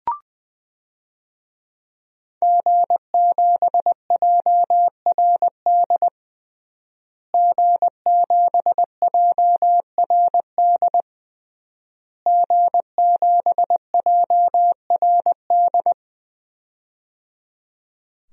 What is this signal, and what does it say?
Morse code, heard as a keyed tone, about 20 wpm, 700 Hz. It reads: G7JRD G7JRD G7JRD
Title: DXCC-100 Call Signs - 5 Characters - Repeated 3x - 20wpm